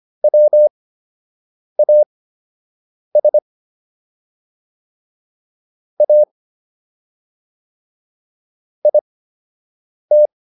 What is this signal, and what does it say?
Morse code, heard as a keyed tone, about 25 wpm, spaced as for 7 wpm, 600 Hz. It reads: WAS A IT